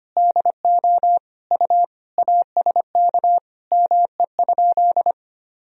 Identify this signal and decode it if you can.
Morse code, heard as a keyed tone, about 25 wpm, 700 Hz. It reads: DO U ASK ME?
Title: Sentences from Top 200 Words and Abbreviations 25wpm